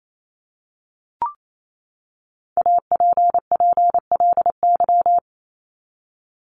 Morse code, heard as a keyed tone, about 28 wpm, 700 Hz. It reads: APPLY
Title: ED Suffix - Top 100 Words with Base Words 28wpm